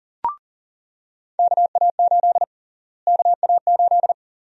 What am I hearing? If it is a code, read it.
Morse code, heard as a keyed tone, about 39 wpm, 700 Hz. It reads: KA8 KA8
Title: USA Call Sign Prefixes - Repeated 2x - 40wpm